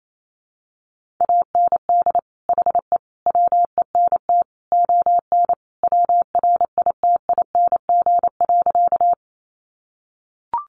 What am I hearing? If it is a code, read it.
Morse code, heard as a keyed tone, about 28 wpm, 700 Hz. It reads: AND HE WENT ON WRITING.
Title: Sentences from Top 800 Words 28wpm